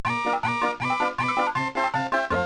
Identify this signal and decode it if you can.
There is no code.